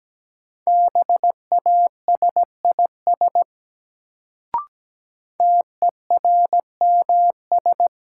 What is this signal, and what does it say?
Morse code, heard as a keyed tone, about 17 wpm, 700 Hz. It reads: BASIS TERMS
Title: Sets of 3 Words - New Word Drill - 17wpm